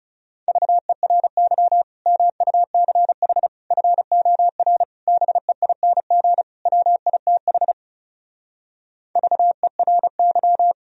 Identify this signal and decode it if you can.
Morse code, heard as a keyed tone, about 37 wpm, 700 Hz. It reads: VERY MUCH FOR BEING WITH VERY
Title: Speed-Racing - Sets of 5 Words 30wpm